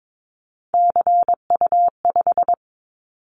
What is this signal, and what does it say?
Morse code, heard as a keyed tone, about 22 wpm, 700 Hz. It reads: CU5
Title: Callsign Prefixes for Top 100 DXCC - 22wpm